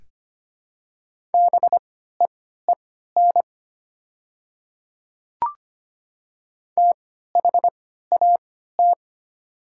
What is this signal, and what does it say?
Morse code, heard as a keyed tone, about 25 wpm, spaced as for 14 wpm, 700 Hz. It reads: BEEN THAT
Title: 3x Character Spacing - Top 100 Words - 25wpm